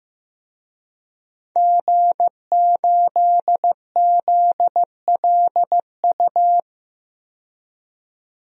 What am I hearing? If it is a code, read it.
Morse code, heard as a keyed tone, about 16 wpm, 700 Hz. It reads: G8ZLU
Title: DXCC-100 Call Signs - 5 Characters - 15wpm